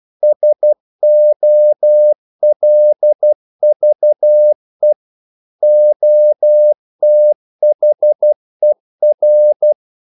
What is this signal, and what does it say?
Morse code, heard as a keyed tone, about 12 wpm, 600 Hz. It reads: SOLVE OTHER
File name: CW / Practice / EnglishQuotations0112